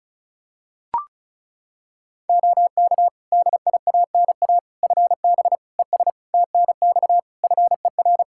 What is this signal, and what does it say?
Morse code, heard as a keyed tone, about 34 wpm, 700 Hz. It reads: OK DIANA FB ES TNX FER
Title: Practice QSO - Exchange 2 - Info - 35wpm